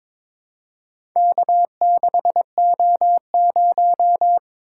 Morse code, heard as a keyed tone, about 22 wpm, 700 Hz. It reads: K6O0
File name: Four Letter-Number Combinations - 22wpm